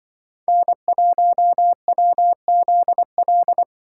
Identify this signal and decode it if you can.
Morse code, heard as a keyed tone, about 24 wpm, 700 Hz. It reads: N1WZL
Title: Speed Racing - USA 1x3 Call Signs 20wpm